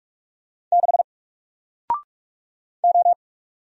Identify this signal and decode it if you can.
Morse code, heard as a keyed tone, about 46 wpm, 700 Hz. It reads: 6 O
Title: Single Letter-Number Alt 45wpm